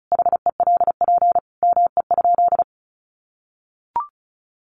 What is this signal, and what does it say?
Morse code, heard as a keyed tone, about 34 wpm, 700 Hz. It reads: HELP ME?
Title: Sentences from Top 200 Words 35wpm